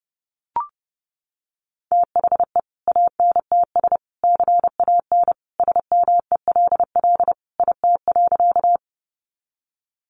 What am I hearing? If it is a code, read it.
Morse code, heard as a keyed tone, about 30 wpm, 700 Hz. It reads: THE ANTS CAN SMELL IT.
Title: Additional Easy Phrases 30wpm